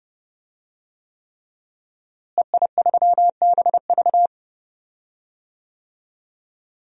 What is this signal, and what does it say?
Morse code, heard as a keyed tone, about 30 wpm, 700 Hz. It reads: EI3BV
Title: DXCC-100 Call Signs - 5 Characters - Repeated 3x - 30wpm